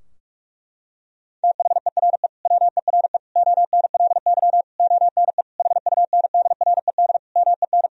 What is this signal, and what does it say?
Morse code, heard as a keyed tone, about 42 wpm, 700 Hz. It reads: THERE WERE ONLY ONE HUNDRED MEN
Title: Sentences from Top 1000 Words - Review Part 2 - 45wpm